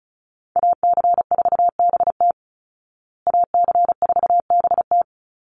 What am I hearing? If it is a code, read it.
Morse code, heard as a keyed tone, about 34 wpm, 700 Hz. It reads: AC4BT AC4BT
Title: Call Signs Repeated 3x - 35wpm